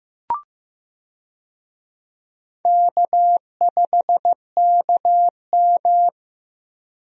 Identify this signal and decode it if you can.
Morse code, heard as a keyed tone, about 15 wpm, 700 Hz. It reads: K5KM
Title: USA Call Signs - 4 Characters - Repeated 3x - 15wpm